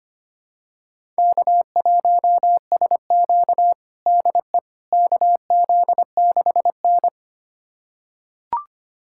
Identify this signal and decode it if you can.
Morse code, heard as a keyed tone, about 25 wpm, 700 Hz. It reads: K1SQ DE KZ6N
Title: USA Callsign DE Callsign - 4 Characters - 25wpm